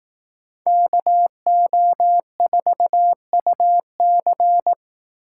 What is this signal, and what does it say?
Morse code, heard as a keyed tone, about 18 wpm, 700 Hz. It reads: KO4UC
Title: Speed Racing - USA 2x2 Call Signs 15wpm